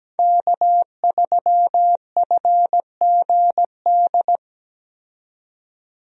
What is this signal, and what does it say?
Morse code, heard as a keyed tone, about 17 wpm, 700 Hz. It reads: K3FGD
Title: USA Call Signs - 5 Characters - 17wpm